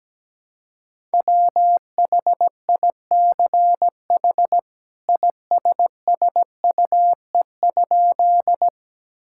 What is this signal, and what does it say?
Morse code, heard as a keyed tone, about 17 wpm, 700 Hz. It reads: WHICH ISSUE?
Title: Sentences from Top 1000 Words - Review Part 6 - 17wpm